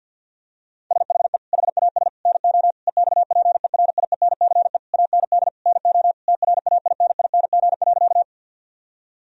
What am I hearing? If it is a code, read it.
Morse code, heard as a keyed tone, about 50 wpm, 700 Hz. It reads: SHE HAS NO EXPERIENCE AND NO TRAINING.
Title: Sentences from Top 1000 Words - Review Part 4 - 50wpm